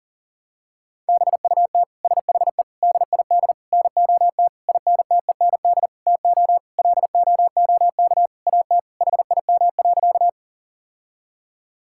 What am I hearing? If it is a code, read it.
Morse code, heard as a keyed tone, about 40 wpm, 700 Hz. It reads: BUT SHE DID NOT INTEND TO LOOK AT HIM.
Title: Sentences from Top 1600 Words 40wpm